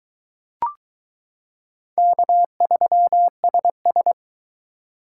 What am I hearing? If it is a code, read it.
Morse code, heard as a keyed tone, about 22 wpm, 700 Hz. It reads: K3SS